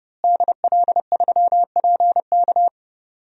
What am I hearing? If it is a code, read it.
Morse code, heard as a keyed tone, about 31 wpm, 700 Hz. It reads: DL3PK
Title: DXCC-100 Call Signs - 5 Characters - Repeated 2x - 30wpm